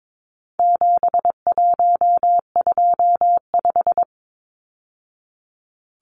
Morse code, heard as a keyed tone, about 22 wpm, 700 Hz. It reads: 7125
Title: Numbers - 4-Digit 22wpm